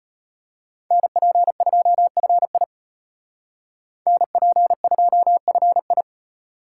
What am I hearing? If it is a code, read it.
Morse code, heard as a keyed tone, about 33 wpm, 700 Hz. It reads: NP2FI NP2FI